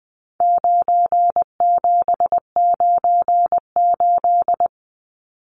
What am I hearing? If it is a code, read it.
Morse code, heard as a keyed tone, about 20 wpm, 700 Hz. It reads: 9798
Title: Numbers - 4-Digit 20wpm